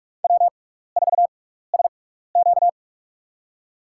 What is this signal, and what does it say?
Morse code, heard as a keyed tone, about 46 wpm, spaced as for 15 wpm, 700 Hz. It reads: W4SQ
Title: USA Call Signs - 6x Character Spacing - 45wpm